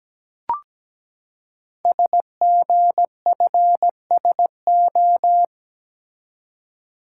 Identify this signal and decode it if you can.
Morse code, heard as a keyed tone, about 17 wpm, 700 Hz. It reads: SGFSO
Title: Code Groups - 5 Letters - 17wpm